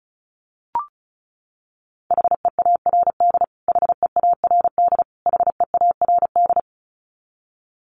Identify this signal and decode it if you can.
Morse code, heard as a keyed tone, about 34 wpm, 700 Hz. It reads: HEARD HEARD HEARD
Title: Top 500 Words Repeated 35wpm